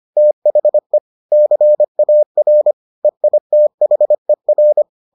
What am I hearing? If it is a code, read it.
Morse code, heard as a keyed tone, about 25 wpm, 600 Hz. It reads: THE CAR EITHER